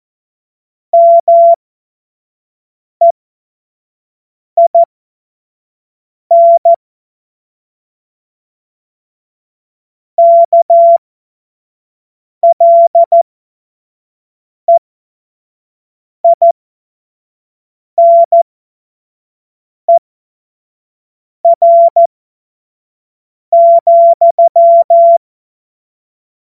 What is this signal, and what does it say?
Morse code, heard as a keyed tone, about 14 wpm, spaced as for 5 wpm, 680 Hz. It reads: MEIN KLEINER,
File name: Morsekurs / Witze / Witz 19. Oktober 2021, 25 F70